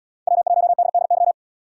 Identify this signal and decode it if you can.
Morse code, heard as a keyed tone, about 77 wpm, 700 Hz. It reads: W1DDY